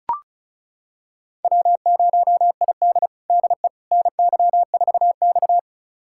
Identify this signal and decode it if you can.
Morse code, heard as a keyed tone, about 36 wpm, 700 Hz. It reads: W0ID DE NY4X
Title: USA Callsign DE Callsign - 4 Characters - 35wpm